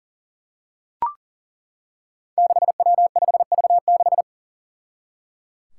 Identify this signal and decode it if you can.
Morse code, heard as a keyed tone, about 37 wpm, 700 Hz. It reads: 6W5V6
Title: Five Challenging Letter-Number Combinations 40wpm